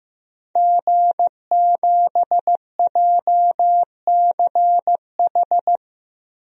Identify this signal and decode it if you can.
Morse code, heard as a keyed tone, about 15 wpm, 700 Hz. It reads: G7JCH
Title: DXCC-100 Call Signs - 5 Characters - 15wpm